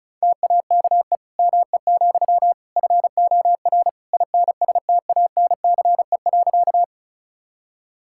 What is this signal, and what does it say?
Morse code, heard as a keyed tone, about 36 wpm, 700 Hz. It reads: TAKE ME, FOR INSTANCE.
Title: Sentences from Top 1000 Words - Review Part 5 - 35wpm